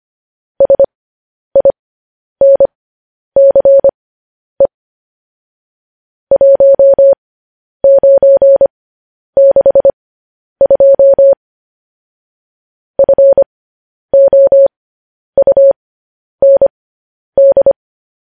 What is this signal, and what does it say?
Morse code, heard as a keyed tone, about 25 wpm, spaced as for 10 wpm, 550 Hz. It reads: SINCE 1962 FOUND
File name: podcast / News headlines-1x-25x10